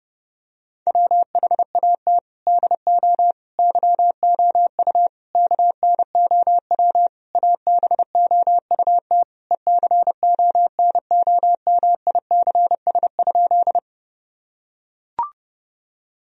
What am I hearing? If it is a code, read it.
Morse code, heard as a keyed tone, about 32 wpm, 700 Hz. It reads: WHAT DO YOU KNOW ABOUT ECONOMICS?
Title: Sentences from Top 1000 Words - Review Part 3 - 30wpm